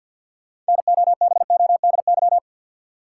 Morse code, heard as a keyed tone, about 50 wpm, 700 Hz. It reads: NOBODY
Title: Sentences from Top 1600 Words - New Word Drill Alt - 50wpm